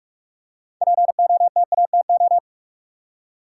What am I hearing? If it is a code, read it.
Morse code, heard as a keyed tone, about 47 wpm, 700 Hz. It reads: POTATO
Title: Six Letter Words - 45wpm